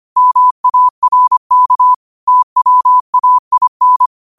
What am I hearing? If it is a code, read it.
Morse code, heard as a keyed tone, about 25 wpm, 1000 Hz. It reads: MARK TWAIN
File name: qotdfiles / 25wpm